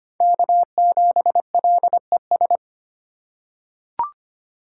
Morse code, heard as a keyed tone, about 25 wpm, 700 Hz. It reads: K7LES
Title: USA Call Signs - 3x Character Spacing - 25wpm